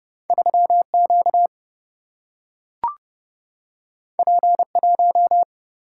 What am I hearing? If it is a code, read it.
Morse code, heard as a keyed tone, about 30 wpm, 700 Hz. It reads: 3 Q P 1
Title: Single-Letter Single-Number 30wpm